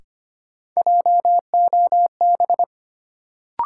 Morse code, heard as a keyed tone, about 25 wpm, 700 Hz. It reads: JOB
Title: Sentences from Top 600 Words - New Word Drill - 25wpm